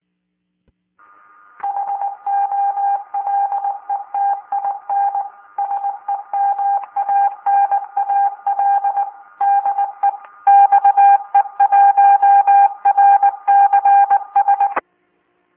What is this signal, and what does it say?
Morse code, heard as a keyed tone, about 20 wpm, 800 Hz. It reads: HOLETIN SEMANAL DE XE1RCS